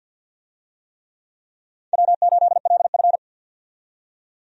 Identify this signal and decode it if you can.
Morse code, heard as a keyed tone, about 52 wpm, 700 Hz. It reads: W8LF
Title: USA Call Signs - 4 Characters - 50wpm